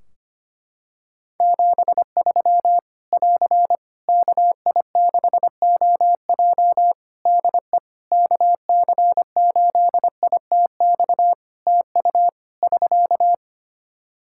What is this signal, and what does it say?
Morse code, heard as a keyed tone, about 25 wpm, 700 Hz. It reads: 73 <AR> KI6OJ DE KC8ITX TU <SK>
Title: K1USN SST QSO Practice 25wpm